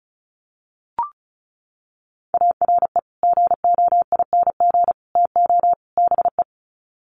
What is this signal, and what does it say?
Morse code, heard as a keyed tone, about 34 wpm, 700 Hz. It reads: ARE GOING TO BE
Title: Sets of 4 Words 35wpm